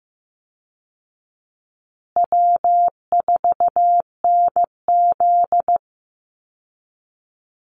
Morse code, heard as a keyed tone, about 15 wpm, 700 Hz. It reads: W4NZ
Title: Easy Call Signs Repeated 3x - 15wpm